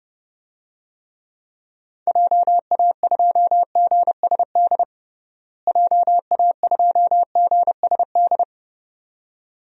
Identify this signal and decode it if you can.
Morse code, heard as a keyed tone, about 30 wpm, 700 Hz. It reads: JA2GSD JA2GSD
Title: DXCC-100 Call Signs - 6 Characters - Repeated 2x - 30wpm